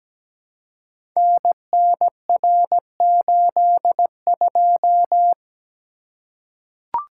Morse code, heard as a keyed tone, about 17 wpm, 700 Hz. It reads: NNR82
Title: Code Groups - 5 Letter:Numbers - 17wpm